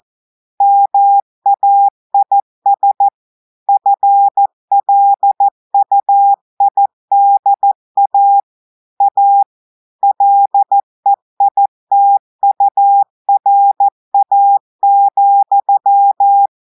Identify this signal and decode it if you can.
Morse code, heard as a keyed tone, about 14 wpm, 800 Hz. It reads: MAIS FLUIDA A LEITURA,
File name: audios / portugais / portugais texte en clair 1 vitesse 14 mots minutes